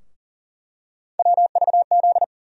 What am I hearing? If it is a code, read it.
Morse code, heard as a keyed tone, about 38 wpm, 700 Hz. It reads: WVZ
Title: USA Call Sign Postfixes - 3 Letters 40wpm